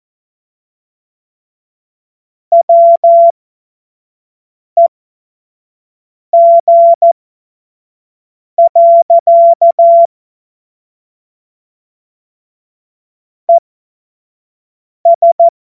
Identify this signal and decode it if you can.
Morse code, heard as a keyed tone, about 14 wpm, spaced as for 5 wpm, 680 Hz. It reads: WEG. ES
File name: Morsekurs / Witze / Witz 27. Juni 2023 25 F70